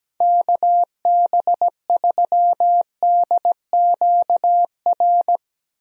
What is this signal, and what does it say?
Morse code, heard as a keyed tone, about 17 wpm, 700 Hz. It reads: KB3DQR
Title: USA Call Signs - 6 Characters - Repeated 3x - 17wpm